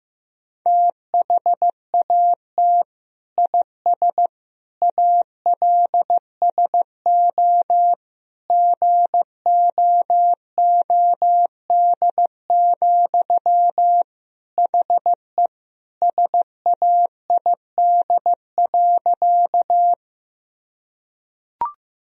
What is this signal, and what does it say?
Morse code, heard as a keyed tone, about 15 wpm, 700 Hz. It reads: THAT IS ALSO GOOD, HE SAID.
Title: Sentences from Top 1000 Words - Review Part 1 - 15wpm